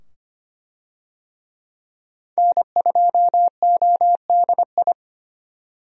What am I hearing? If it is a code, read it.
Morse code, heard as a keyed tone, about 25 wpm, 700 Hz. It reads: N2ODI
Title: USA Call Signs - 5 Characters - 25wpm